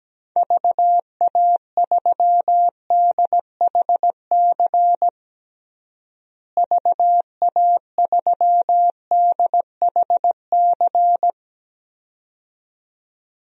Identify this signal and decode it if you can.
Morse code, heard as a keyed tone, about 17 wpm, 700 Hz. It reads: VA3DHC VA3DHC
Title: DXCC-100 Call Signs - 6 Characters - Repeated 2x - 17wpm